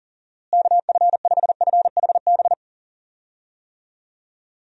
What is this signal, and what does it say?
Morse code, heard as a keyed tone, about 40 wpm, 700 Hz. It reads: KF5FHB
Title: USA Call Signs - 6 Characters - Repeated 3x - 40wpm